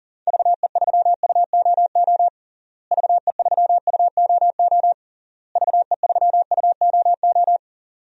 Morse code, heard as a keyed tone, about 40 wpm, 700 Hz. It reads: VE3UOO VE3UOO VE3UOO
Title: DXCC-100 Call Signs - 6 Characters - Repeated 3x - 40wpm